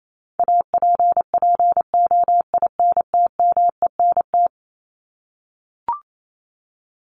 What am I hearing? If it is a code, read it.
Morse code, heard as a keyed tone, about 28 wpm, 700 Hz. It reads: APPOINTMENT
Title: Top 1751-2000 Words 28wpm